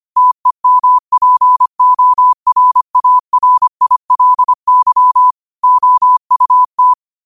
Morse code, heard as a keyed tone, about 25 wpm, 1000 Hz. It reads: TEMPORARILY OUT
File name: qotdfiles / 25wpm